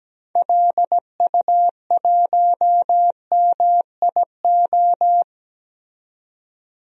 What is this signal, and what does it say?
Morse code, heard as a keyed tone, about 17 wpm, 700 Hz. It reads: LU1MIO
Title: DXCC-100 Call Signs - 6 Characters - 17wpm